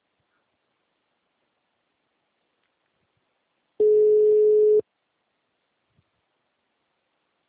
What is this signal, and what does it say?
Ringback tone (Europe, 425 Hz).
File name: out- -2012- - - .42383